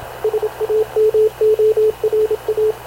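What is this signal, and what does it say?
Morse code, heard as a keyed tone, about 27 wpm, 430 Hz, with radio noise. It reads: SAMORA
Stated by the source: Havana Radio CLA press (PX) broadcast, recorded by Radio Officer Paul Dumesnil aboard mv CNAV QUEST/CZDO.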